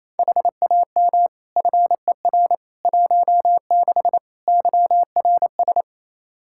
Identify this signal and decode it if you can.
Morse code, heard as a keyed tone, about 28 wpm, 700 Hz. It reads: HAM FER 16 YRS
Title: Practice QSO - Exchange 3 - More Info - 28wpm